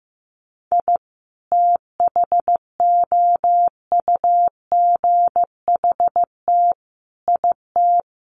Morse code, heard as a keyed tone, about 15 wpm, 700 Hz. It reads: I THOUGHT IT
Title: Sets of 5 Words Encore 15wpm